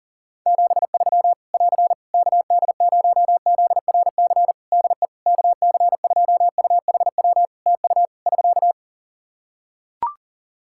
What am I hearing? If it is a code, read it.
Morse code, heard as a keyed tone, about 40 wpm, 700 Hz. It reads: 73 <AR> KD0ZRC DE KC2UHW TU <SK>
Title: Practice QSO - Exchange 4 - Ending - 40wpm